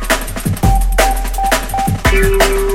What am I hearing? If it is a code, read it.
There is no code.